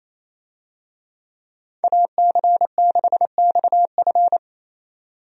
Morse code, heard as a keyed tone, about 28 wpm, 700 Hz. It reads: AC6XF
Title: USA Call Signs - 5 Characters - 28wpm